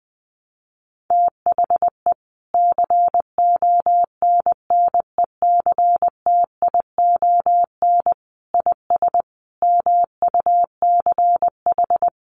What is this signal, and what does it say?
Morse code, heard as a keyed tone, about 20 wpm, 700 Hz. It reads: THE CONNECTION IS MUCH
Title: Sentences from Top 1300 Words 20wpm